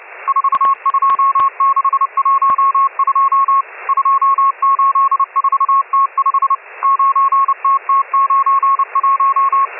Morse code, heard as a keyed tone, about 28 wpm, 1100 Hz, with radio noise. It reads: 42612 284T5 9TT91
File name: M12- -1010utc